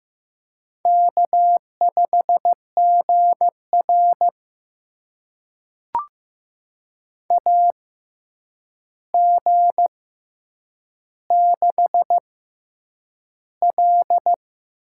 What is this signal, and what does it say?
Morse code, heard as a keyed tone, about 15 wpm, spaced as for 8 wpm, 700 Hz. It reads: K5GR AG6L
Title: USA Call Signs - 4 Characters - 6x Character Spacing - 15wpm